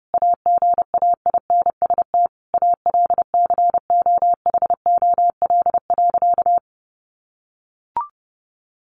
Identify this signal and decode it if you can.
Morse code, heard as a keyed tone, about 30 wpm, 700 Hz. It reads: AGAINST ALCOHOL.
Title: Sentences from Top 1400 Words 30wpm